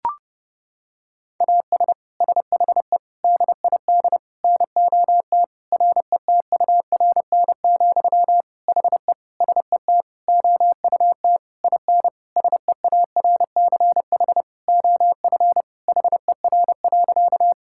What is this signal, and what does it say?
Morse code, heard as a keyed tone, about 32 wpm, 700 Hz. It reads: AS SHE DID NOT RETURN, HE SET OUT IN SEARCH OF HER.
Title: Sentences from Top 1000 Words - Review Part 5 - 30wpm